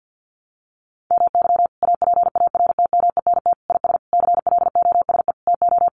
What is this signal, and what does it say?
Morse code, heard as a keyed tone, about 50 wpm, 700 Hz. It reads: MY APARTMENT IS CLOSE TO
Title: Sentences from Top 900 Words 50wpm